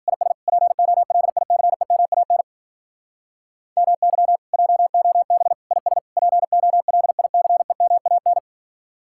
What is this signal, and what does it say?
Morse code, heard as a keyed tone, about 46 wpm, 700 Hz. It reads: IS POLICEMAN MY JOB IS POLICEMAN
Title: Speed-Racing Common QSO Phrases Encore - 40wpm